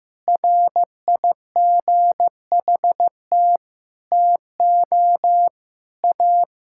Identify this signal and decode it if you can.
Morse code, heard as a keyed tone, about 15 wpm, 700 Hz. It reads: RIGHT TO A
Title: Sentences from Top 1000 Words - Review Part 6 - 15wpm